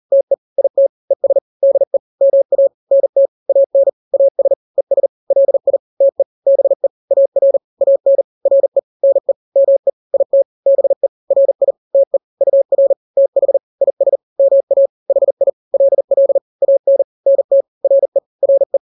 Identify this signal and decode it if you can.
Morse code, heard as a keyed tone, about 40 wpm, 550 Hz. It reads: TE IT ES DE MA NT AN AS ES LI TE BE AR AN RE NE ME IT BE RI TE UR TH IS MA HI LL AN NT RE RE
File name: n-grams / 2-grams 40 35 WPM